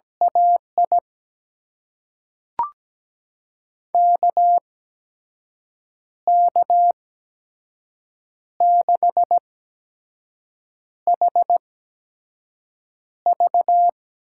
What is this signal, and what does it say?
Morse code, heard as a keyed tone, about 17 wpm, spaced as for 5 wpm, 700 Hz. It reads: AI KK6HV
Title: USA Call Signs - 6 Characters - 8x Character Spacing - 17wpm